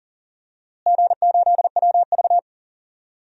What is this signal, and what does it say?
Morse code, heard as a keyed tone, about 40 wpm, 700 Hz. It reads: G8WV
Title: DXCC-100 Call Signs - 4 Characters - Repeated 3x - 40wpm